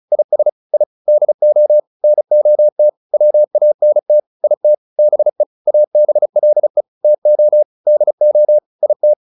Morse code, heard as a keyed tone, about 34 wpm, 600 Hz. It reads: IS I DO NOT WANT IT BE ABLE TO DO IT